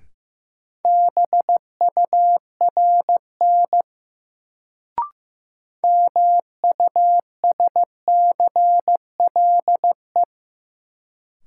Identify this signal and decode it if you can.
Morse code, heard as a keyed tone, about 15 wpm, 700 Hz. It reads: BURN MUSCLE
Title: Sentences from Top 1300 Words - New Word Drill Alt - 15wpm